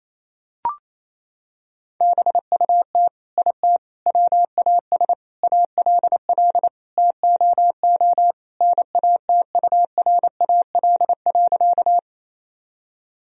Morse code, heard as a keyed tone, about 28 wpm, 700 Hz. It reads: BUT IT WAS ALL TOO NATURAL.
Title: Sentences from Top 1000 Words - Review Part 4 - 28wpm